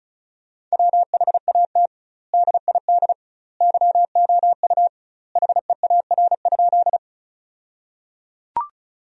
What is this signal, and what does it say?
Morse code, heard as a keyed tone, about 36 wpm, 700 Hz. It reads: WHAT DID YOU HEAR?
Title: 2x Word-Spacing - Sentences from Top 300 Words - 35wpm